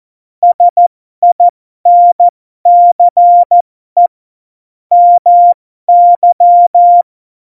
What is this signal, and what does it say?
Morse code, heard as a keyed tone, about 14 wpm, spaced as for 12 wpm, 700 Hz. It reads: SINCE MY
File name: Morsekurs / Witze